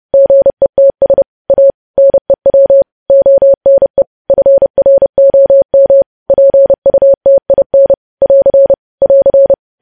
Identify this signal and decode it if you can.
Morse code, heard as a keyed tone, about 30 wpm, 550 Hz. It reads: GETS A NEW ONE FROM PUTIN <AR> <AR>
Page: news headlines in morse code at 30 wpm